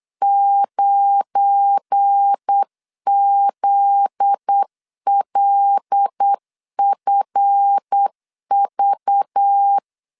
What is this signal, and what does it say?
Morse code, heard as a keyed tone, about 8 wpm, 800 Hz. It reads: 9ZLFV
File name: LEZIONI / 7 GF9Z / 7 GF9Z 8wpm 3213